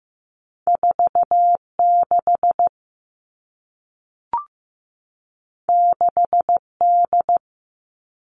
Morse code, heard as a keyed tone, about 15 wpm, 700 Hz. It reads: 4 6 6 D
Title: Two Challenging Letter-Number Combinations 15wpm